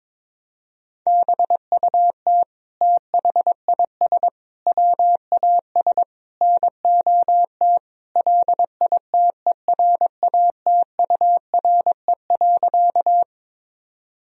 Morse code, heard as a keyed tone, about 22 wpm, 700 Hz. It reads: BUT THIS WAS NOT LITERATURE.
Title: Sentences from Top 1000 Words - Review Part 3 - 22wpm